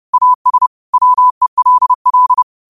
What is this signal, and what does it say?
Morse code, heard as a keyed tone, about 30 wpm, 1000 Hz. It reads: AS WELL
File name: qotdfiles / 30wpm